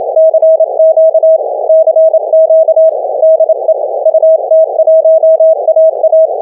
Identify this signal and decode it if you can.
Morse code, heard as a keyed tone, about 27 wpm, 640 Hz. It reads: CQ CQ DE UT1AA